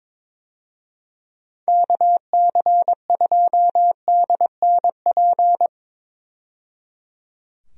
Morse code, heard as a keyed tone, about 23 wpm, 700 Hz. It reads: KC2DNP
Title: USA Call Signs - 6 Characters - 22wpm